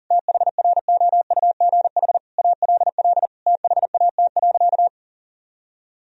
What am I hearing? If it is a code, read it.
Morse code, heard as a keyed tone, about 37 wpm, 700 Hz. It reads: THROUGH ALL THAT.
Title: Sentences from Top 1000 Words - Review Part 1 - 40wpm